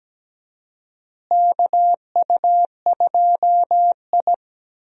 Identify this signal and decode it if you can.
Morse code, heard as a keyed tone, about 17 wpm, 700 Hz. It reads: KU2I
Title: USA Call Signs - 4 Characters - Repeated 2x - 17wpm